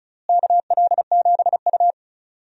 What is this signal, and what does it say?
Morse code, heard as a keyed tone, about 34 wpm, 700 Hz. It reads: KL7U
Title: DXCC-100 Call Signs - 4 Characters - Repeated 2x - 35wpm